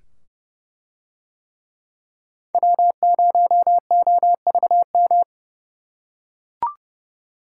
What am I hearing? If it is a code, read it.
Morse code, heard as a keyed tone, about 28 wpm, 700 Hz. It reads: W0OVM